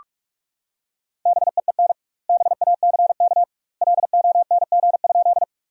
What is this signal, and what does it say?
Morse code, heard as a keyed tone, about 41 wpm, 700 Hz. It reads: BEEN BACK LONG?
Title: 2x Word-Spacing - Sentences from Top 200 Words - 45wpm